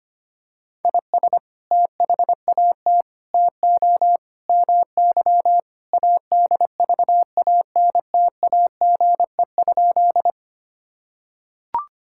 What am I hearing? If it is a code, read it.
Morse code, heard as a keyed tone, about 25 wpm, 700 Hz. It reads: IS THAT TO MY ADVANTAGE?
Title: Sentences from Top 1000 Words - Review Part 6 - 25wpm